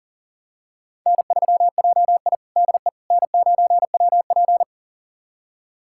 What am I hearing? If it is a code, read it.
Morse code, heard as a keyed tone, about 40 wpm, 700 Hz. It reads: N3JI DE N9WP
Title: USA Callsign DE Callsign - 4 Characters - 40wpm